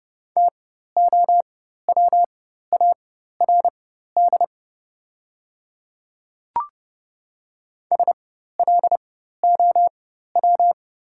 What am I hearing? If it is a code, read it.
Morse code, heard as a keyed tone, about 30 wpm, spaced as for 14 wpm, 700 Hz. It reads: TOWARD SLOW
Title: 4x Character Spacing - Top 500 Words - 30wpm